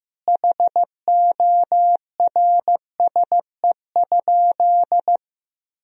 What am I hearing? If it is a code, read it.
Morse code, heard as a keyed tone, about 15 wpm, 700 Hz. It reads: HORSE?
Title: Sentences from Top 1000 Words - Review Part 1 - 15wpm